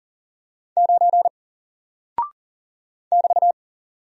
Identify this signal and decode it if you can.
Morse code, heard as a keyed tone, about 40 wpm, 700 Hz. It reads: 9 <BT>
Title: Single Character (Letter:Number:Punctuation:Prosign) 40wpm